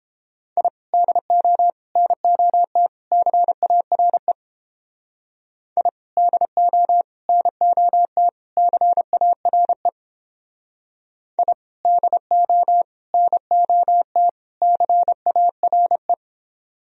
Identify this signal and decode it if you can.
Morse code, heard as a keyed tone, about 30 wpm, 700 Hz. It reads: I DO NOT CARE I DO NOT CARE I DO NOT CARE